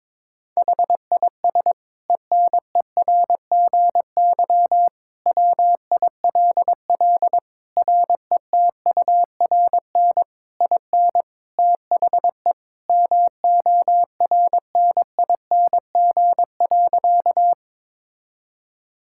Morse code, heard as a keyed tone, about 22 wpm, 700 Hz. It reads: HIS ENERGY WILL RETURN IN THE MORNING.
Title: Sentences from Top 1000 Words - Review Part 6 - 22wpm